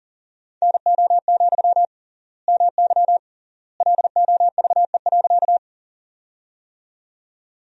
Morse code, heard as a keyed tone, about 40 wpm, 700 Hz. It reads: NO, MY LOVE.
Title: 3x Word-Spacing - Sentences from Top 400 Words - 40wpm